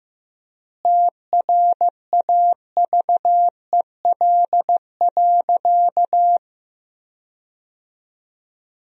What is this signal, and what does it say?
Morse code, heard as a keyed tone, about 15 wpm, 700 Hz. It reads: TRAVEL.
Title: 4x Word-Spacing - Sentences from Top 400 Words - 15wpm